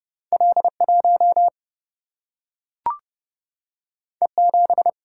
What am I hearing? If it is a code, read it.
Morse code, heard as a keyed tone, about 30 wpm, 700 Hz. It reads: L1 E7
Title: Single-Letter Single-Number 30wpm